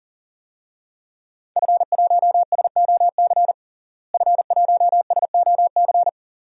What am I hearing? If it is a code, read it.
Morse code, heard as a keyed tone, about 40 wpm, 700 Hz. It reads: F1SOC F1SOC
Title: DXCC-100 Call Signs - 5 Characters - Repeated 2x - 40wpm